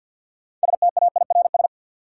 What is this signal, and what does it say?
Morse code, heard as a keyed tone, about 46 wpm, 700 Hz. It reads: STAIRS
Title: Sets of 5 Words Encore - New Word Drill - 50wpm